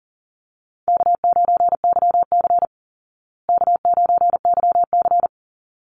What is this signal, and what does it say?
Morse code, heard as a keyed tone, about 39 wpm, 700 Hz. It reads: K9YC K9YC
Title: Easy Call Signs Repeated 2x - 40wpm